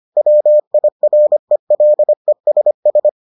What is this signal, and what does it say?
Morse code, heard as a keyed tone, about 25 wpm, 600 Hz. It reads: WIRELESS